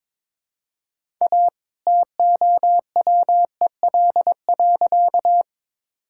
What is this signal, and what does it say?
Morse code, heard as a keyed tone, about 22 wpm, 700 Hz. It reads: A TOWEL.
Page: morse-code-ninja-practice-22wpm